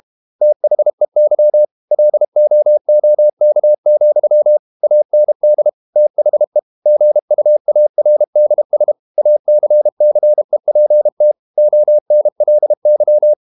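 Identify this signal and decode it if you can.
Morse code, heard as a keyed tone, about 32 wpm, 600 Hz. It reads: THEY LOOK, AND THE GUARDS ACCEPT ONLY
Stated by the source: Morse code practice at 32 wpm from CW med Gnister, reading Ken Follett's novel Never in English.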